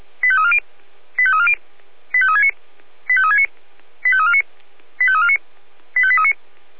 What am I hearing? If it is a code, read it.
Five-tone selective call (ZVEI), heard as quick runs of five tones, tones of about 70 ms. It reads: 85340 85340 85369 85369 85349 85349 85839
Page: SDR/ZVEI